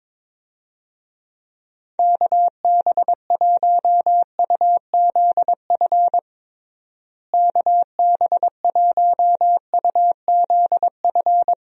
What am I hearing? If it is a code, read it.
Morse code, heard as a keyed tone, about 22 wpm, 700 Hz. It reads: KB1UZF KB1UZF
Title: USA Call Signs - 6 Characters - Repeated 2x - 22wpm